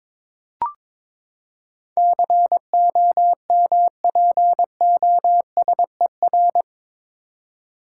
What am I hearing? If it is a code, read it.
Morse code, heard as a keyed tone, about 22 wpm, 700 Hz. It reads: COMPOSER
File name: practice / 22wpm / Eight Letter Words - 22wpm